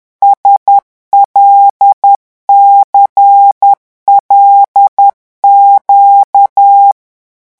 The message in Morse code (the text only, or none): SLCLQ